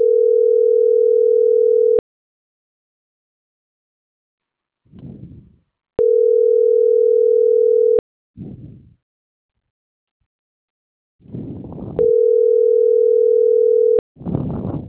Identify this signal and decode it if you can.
Ringback tone (North America), 440 + 480 Hz.